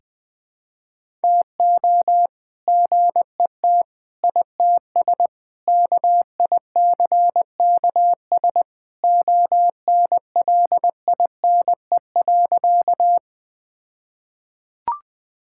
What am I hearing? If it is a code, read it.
Morse code, heard as a keyed tone, about 21 wpm, 700 Hz. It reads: TO GET ITS KICKS ONLINE.
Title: All-Age Appropriate Jokes - 20wpm